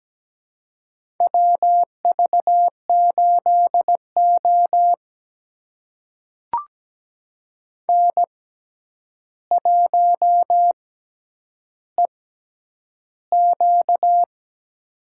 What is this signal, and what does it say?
Morse code, heard as a keyed tone, about 17 wpm, spaced as for 9 wpm, 700 Hz. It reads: WV8O N1EQ
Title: USA Call Signs - 4 Characters - 6x Character Spacing - 17wpm